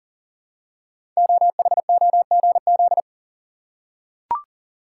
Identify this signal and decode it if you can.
Morse code, heard as a keyed tone, about 42 wpm, 690 Hz. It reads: OHOGZ
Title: Code Groups - 5 Letters - 40wpm